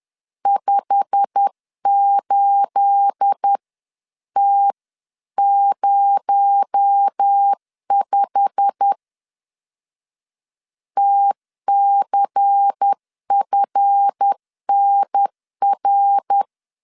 Morse code, heard as a keyed tone, about 10 wpm, 800 Hz. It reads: 58T05 TCFNR
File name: LEZIONI / 9 28 10wpm 2842